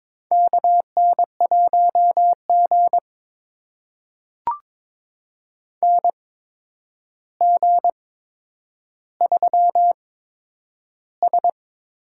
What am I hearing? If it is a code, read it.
Morse code, heard as a keyed tone, about 22 wpm, spaced as for 9 wpm, 700 Hz. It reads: KN1G NG3S
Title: USA Call Signs - 4 Characters - 8x Character Spacing - 22wpm